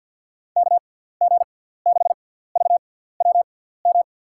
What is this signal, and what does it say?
Morse code, heard as a keyed tone, about 52 wpm, spaced as for 17 wpm, 700 Hz. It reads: KG6VWM
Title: USA Call Signs - 6 Characters - 6x Character Spacing - 50wpm